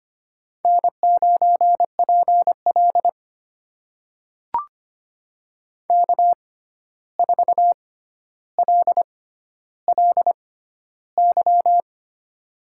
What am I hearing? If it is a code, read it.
Morse code, heard as a keyed tone, about 25 wpm, spaced as for 12 wpm, 700 Hz. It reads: N9PL K4LLY
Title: USA Call Signs - 6x Character Spacing - 25wpm